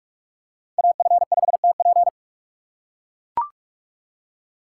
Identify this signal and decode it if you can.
Morse code, heard as a keyed tone, about 43 wpm, 700 Hz. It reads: AF5TP
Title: USA Call Signs - 5 Characters - 2x Character Spacing - 45wpm